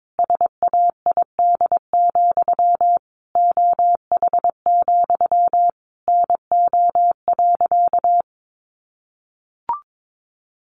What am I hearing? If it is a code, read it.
Morse code, heard as a keyed tone, about 22 wpm, 700 Hz. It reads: SAID, OH, NO.
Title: Sentences from Top 1200 Words 22wpm